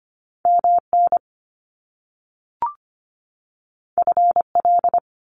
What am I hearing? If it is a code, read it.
Morse code, heard as a keyed tone, about 25 wpm, 700 Hz. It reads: MN FL
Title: US States 25wpm